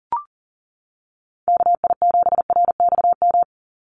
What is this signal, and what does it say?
Morse code, heard as a keyed tone, about 39 wpm, 690 Hz. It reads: KI7RXM
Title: USA Call Signs 40wpm